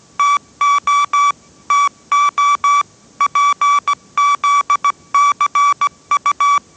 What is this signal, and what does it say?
Morse code, heard as a keyed tone, about 17 wpm, 1150 Hz. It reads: TO TO PZCU
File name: 18wpm